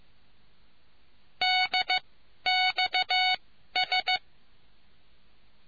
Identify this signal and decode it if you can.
Morse code, heard as a keyed tone, about 15 wpm, spaced as for 11 wpm, 740 Hz. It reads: DXS